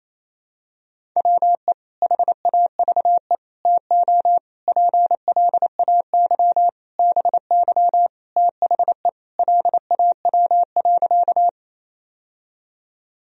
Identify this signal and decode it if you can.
Morse code, heard as a keyed tone, about 28 wpm, 700 Hz. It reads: WE HAVE TO PLAY BY THE LAW.
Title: Sentences from Top 1000 Words - Review Part 3 - 28wpm